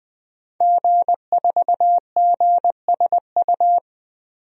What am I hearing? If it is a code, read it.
Morse code, heard as a keyed tone, about 20 wpm, 700 Hz. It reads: G4GSU
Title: DXCC-100 Call Signs - 5 Characters - Repeated 3x - 20wpm